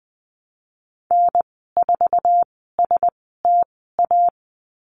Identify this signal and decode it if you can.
Morse code, heard as a keyed tone, about 20 wpm, spaced as for 14 wpm, 700 Hz. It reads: N4STA
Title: 048 Lesson 9g - Number 4 - Call Signs 20wpm